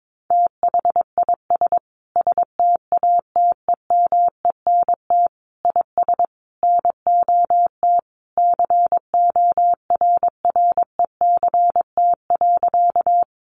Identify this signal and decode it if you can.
Morse code, heard as a keyed tone, about 22 wpm, 700 Hz. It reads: THIS STATEMENT IS NOT CORRECT.
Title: Sentences from Top 1200 Words 22wpm